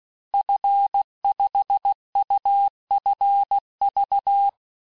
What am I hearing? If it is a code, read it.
Morse code, heard as a keyed tone, about 16 wpm, 800 Hz. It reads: F5UFV
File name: cours-du-lundi files / cours-1592--teste-FIN-de-QSO-vitesse-15mpm